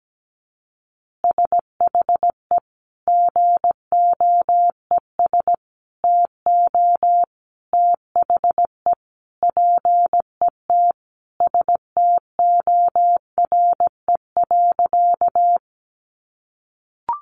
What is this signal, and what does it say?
Morse code, heard as a keyed tone, about 17 wpm, 700 Hz. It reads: SHE GOES TO THE PET STORE.
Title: Easy Phrases 17wpm